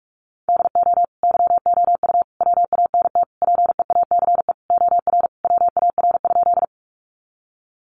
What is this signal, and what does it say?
Morse code, heard as a keyed tone, about 44 wpm, 700 Hz. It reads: DO YOU WANT PEACE OR WAR?
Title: Sentences from Top 1500 Words 45wpm